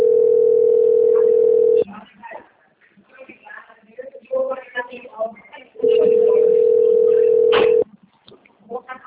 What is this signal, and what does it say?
Ringback tone (North America), 440 + 480 Hz, with speech.
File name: rg-1600-unknown- - - .58259